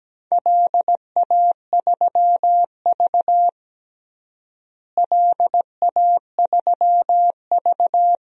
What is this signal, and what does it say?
Morse code, heard as a keyed tone, about 17 wpm, 700 Hz. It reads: LA3V LA3V
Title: DXCC-100 Call Signs - 4 Characters - Repeated 2x - 17wpm